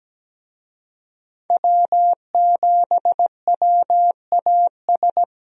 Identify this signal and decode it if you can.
Morse code, heard as a keyed tone, about 17 wpm, 700 Hz. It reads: W7WAS
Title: USA Call Signs - 5 Characters - Repeated 3x - 17wpm